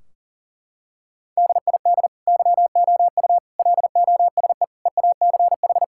Morse code, heard as a keyed tone, about 37 wpm, 700 Hz. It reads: DID YOU LOSE EACH
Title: Sentences from Top 1600 Words 40wpm